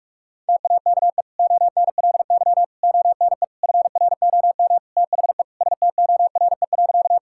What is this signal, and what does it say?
Morse code, heard as a keyed tone, about 46 wpm, 700 Hz. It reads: TAKE ONLY ONE FROM THE STORE.
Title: Sentences from Top 1000 Words - Review Part 5 - 45wpm